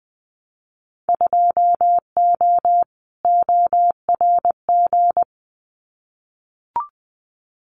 Morse code, heard as a keyed tone, about 20 wpm, 700 Hz. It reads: 2O ORG
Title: Winter Field Day Exchanges 20wpm